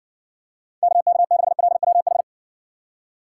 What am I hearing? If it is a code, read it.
Morse code, heard as a keyed tone, about 62 wpm, 700 Hz. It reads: KK6LWH